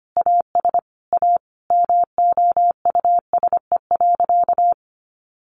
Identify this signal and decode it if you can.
Morse code, heard as a keyed tone, about 25 wpm, 700 Hz. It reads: AS A MOUSE.
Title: Sentences from Top 700 Words 25wpm